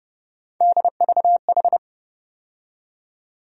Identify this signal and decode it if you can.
Morse code, heard as a keyed tone, about 30 wpm, 700 Hz. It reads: DVH
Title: USA Call Sign Postfixes - 3 Letters 30wpm